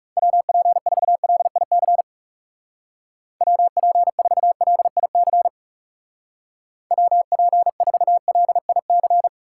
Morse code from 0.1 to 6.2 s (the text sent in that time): WP4LIC WP4LIC